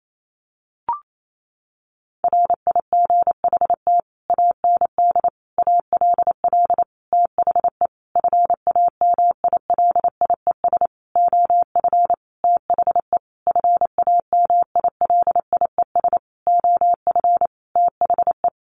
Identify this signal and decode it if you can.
Morse code, heard as a keyed tone, about 28 wpm, 700 Hz. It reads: RIGHT AND ALL THE FAMILIES OF THE FAMILIES OF THE
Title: A Sound Of Thunder 28wpm